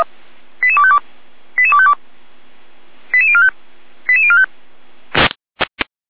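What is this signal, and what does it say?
Five-tone selective call (ZVEI), heard as quick runs of five tones, tones of about 70 ms. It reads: 88252 88252 88045 88045
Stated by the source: Five-tone ZVEI selective call received by SDR, archived by the Orlamünde fire brigade (feuerwehr-orlamuende.org).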